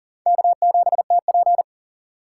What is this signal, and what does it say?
Morse code, heard as a keyed tone, about 40 wpm, 700 Hz. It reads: K7TP
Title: USA Call Signs - 4 Characters - 4x Character Spacing - 40wpm